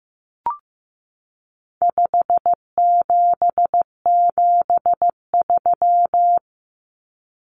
Morse code, heard as a keyed tone, about 15 wpm, 700 Hz. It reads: 5773
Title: Common Cut Numbers - 4-Digit 15wpm